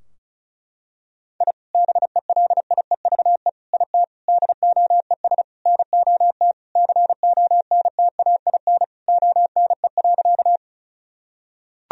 Morse code, heard as a keyed tone, about 36 wpm, 700 Hz. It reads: I BELIEVE IT DOES NOT CONTAIN ONE.
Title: Sentences from Top 1000 Words - Review Part 2 - 35wpm